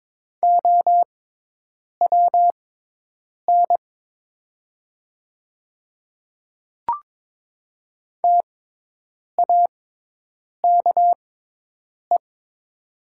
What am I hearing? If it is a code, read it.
Morse code, heard as a keyed tone, about 22 wpm, spaced as for 8 wpm, 700 Hz. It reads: OWN TAKE
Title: 6x Character Spacing - Top 300 Words - 22wpm